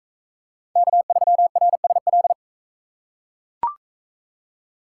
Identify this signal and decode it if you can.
Morse code, heard as a keyed tone, about 40 wpm, 700 Hz. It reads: K3RSL